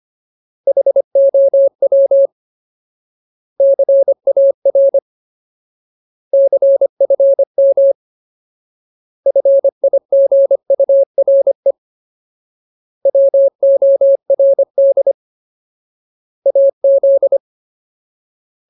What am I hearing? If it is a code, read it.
Morse code, heard as a keyed tone, about 25 wpm, 550 Hz. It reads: HOW CAR CFM FIGURE WORD AZ
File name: Ultimate Word List / 25WPM Ultimate